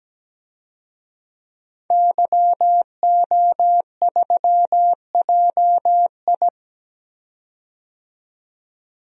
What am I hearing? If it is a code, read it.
Morse code, heard as a keyed tone, about 17 wpm, 700 Hz. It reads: YO3JI
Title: DXCC-100 Call Signs - 5 Characters - 17wpm